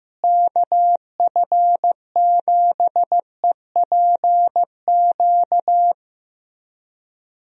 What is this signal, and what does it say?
Morse code, heard as a keyed tone, about 15 wpm, 700 Hz. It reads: KF7EPQ
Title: USA Call Signs - 6 Characters - Repeated 2x - 15wpm